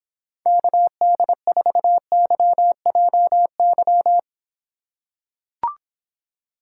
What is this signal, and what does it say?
Morse code, heard as a keyed tone, about 26 wpm, 700 Hz. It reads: KD4YJY